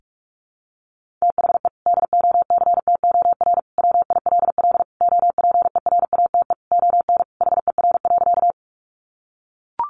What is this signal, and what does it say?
Morse code, heard as a keyed tone, about 44 wpm, 700 Hz. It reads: THE DOCTOR WILL OPERATE ON HER.
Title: Sentences from Top 1500 Words 45wpm